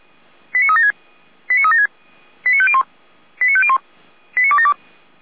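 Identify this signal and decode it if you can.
Five-tone selective call (ZVEI), heard as quick runs of five tones, tones of about 70 ms. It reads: 89376 89376 89501 89501 89272
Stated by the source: ZVEI five-tone selective call 89270 received on SDR.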